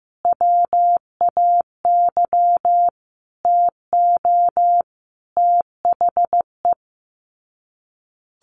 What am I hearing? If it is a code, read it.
Morse code, heard as a keyed tone, about 15 wpm, 700 Hz. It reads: WAY TO THE
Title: Sets of 5 Words 15wpm